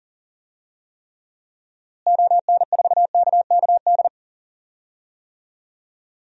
Morse code, heard as a keyed tone, about 40 wpm, 700 Hz. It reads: ON4KKD
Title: DXCC-100 Call Signs - 6 Characters - Repeated 2x - 40wpm